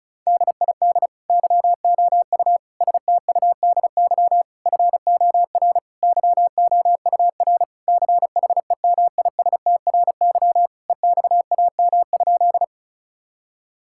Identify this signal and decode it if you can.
Morse code, heard as a keyed tone, about 36 wpm, 700 Hz. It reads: DID YOU STUDY FOR YOUR CHEMISTRY EXAM?
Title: Sentences from Top 1000 Words - Review Part 6 - 35wpm